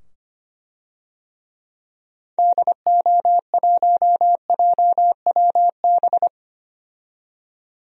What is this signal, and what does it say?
Morse code, heard as a keyed tone, about 25 wpm, 700 Hz. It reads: DO1JWB
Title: DXCC-100 Call Signs - 6 Characters - 25wpm